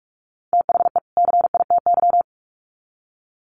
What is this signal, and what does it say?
Morse code, heard as a keyed tone, about 44 wpm, 700 Hz. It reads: THE CITY
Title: Sets of 2 Words 45wpm